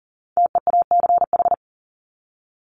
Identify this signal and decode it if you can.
Morse code, heard as a keyed tone, about 40 wpm, 700 Hz. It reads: TEACH
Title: Top 500 Words 40wpm